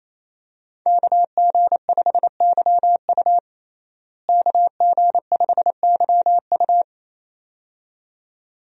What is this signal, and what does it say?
Morse code, heard as a keyed tone, about 28 wpm, 700 Hz. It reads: KG5YU KG5YU
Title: USA Call Signs - 5 Characters - Repeated 2x - 28wpm